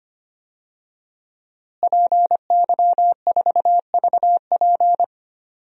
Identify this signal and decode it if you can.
Morse code, heard as a keyed tone, about 25 wpm, 700 Hz. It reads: PY4VP
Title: DXCC-100 Call Signs - 5 Characters - Repeated 3x - 25wpm